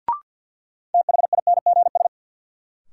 Morse code, heard as a keyed tone, about 53 wpm, 700 Hz. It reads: THINGS
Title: Sentences from Top 1400 Words - New Word Drill Alt - 50wpm